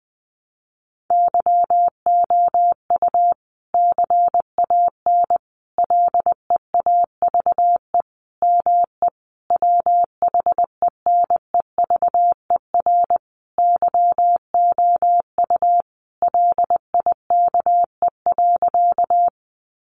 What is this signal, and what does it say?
Morse code, heard as a keyed tone, about 20 wpm, 700 Hz. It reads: YOU CAN LEAVE ME WHENEVER YOU LIKE.
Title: Sentences from Top 1200 Words 20wpm